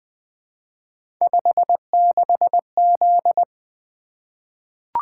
Morse code, heard as a keyed tone, about 20 wpm, 700 Hz. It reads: 56Z
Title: Three Letter-Number Combinations - 20wpm